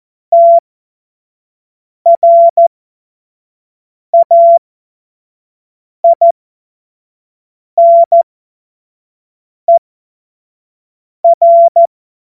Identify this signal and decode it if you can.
Morse code, heard as a keyed tone, about 14 wpm, spaced as for 5 wpm, 680 Hz. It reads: TRAINER
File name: Morsekurs / Witze / Witz 02. Februar 2021, 25 F70